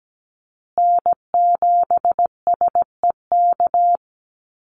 Morse code, heard as a keyed tone, about 17 wpm, 700 Hz. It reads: N7SEK
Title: USA Call Signs Repeated 2x 17wpm